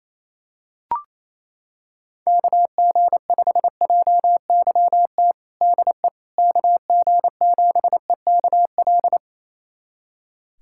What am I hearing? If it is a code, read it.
Morse code, heard as a keyed tone, about 28 wpm, 700 Hz. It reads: KG5JYT DE KG7EKL
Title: USA Callsign DE Callsign 28wpm